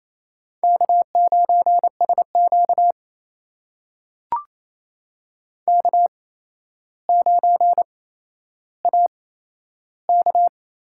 Morse code, heard as a keyed tone, about 28 wpm, spaced as for 12 wpm, 700 Hz. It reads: K9SQ K9AK
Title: USA Call Signs - 4 Characters - 8x Character Spacing - 28wpm